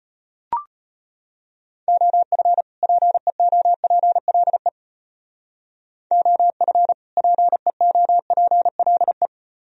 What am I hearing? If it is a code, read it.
Morse code, heard as a keyed tone, about 33 wpm, 700 Hz. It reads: OF PEOPLE OF PEOPLE